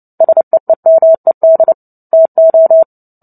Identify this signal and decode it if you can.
Morse code, heard as a keyed tone, about 30 wpm, 650 Hz. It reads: SEEMED TO
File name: Book Two / 03-Chapter 3 - 29 WPM